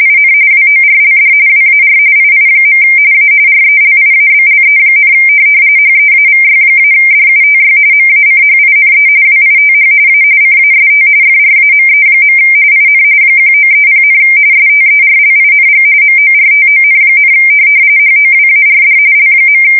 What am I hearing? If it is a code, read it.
RTTY, about 45.45 baud. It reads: IN WORLD WAR II. A RELIABLE, HEAVY-DUTY MACHINE WITH A CAST FRAME. IN 1930, STERLING MORTON, HOWARD L.
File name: fsk170 / teletype-corp